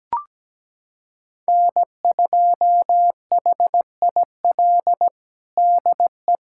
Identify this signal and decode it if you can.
Morse code, heard as a keyed tone, about 17 wpm, 700 Hz. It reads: N2HIL DE
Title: USA Callsign DE Callsign - 5 Characters - 17wpm